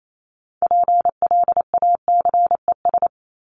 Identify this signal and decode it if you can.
Morse code, heard as a keyed tone, about 28 wpm, 700 Hz. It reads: PLACES
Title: ES:S Suffix - Top 100 Words with Base Words 28wpm